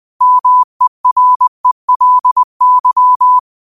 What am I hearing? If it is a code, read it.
Morse code, heard as a keyed tone, about 20 wpm, 1000 Hz. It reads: MERELY
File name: qotdfiles / 20wpm